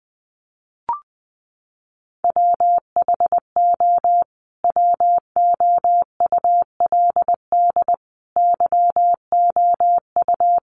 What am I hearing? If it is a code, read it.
Morse code, heard as a keyed tone, about 20 wpm, 700 Hz. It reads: WHO WOULD YOU
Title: Sentences from Top 1300 Words 20wpm